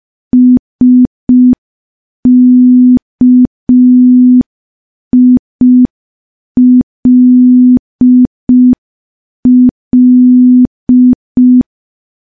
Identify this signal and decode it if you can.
Morse code, heard as a keyed tone, about 5 wpm, 250 Hz. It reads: SKILL
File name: morse / morse-training-05WPM